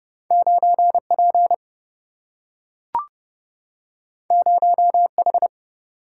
Morse code, heard as a keyed tone, about 30 wpm, 700 Hz. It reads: 9P 0H
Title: Single-Letter Single-Number 30wpm